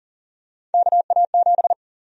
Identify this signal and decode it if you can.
Morse code, heard as a keyed tone, about 40 wpm, 700 Hz. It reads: KA7
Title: USA Call Sign Prefixes - 40wpm